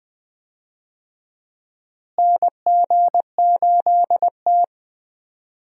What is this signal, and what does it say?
Morse code, heard as a keyed tone, about 20 wpm, 700 Hz. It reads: NG8T
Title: USA Call Signs - 4 Characters - Repeated 2x - 20wpm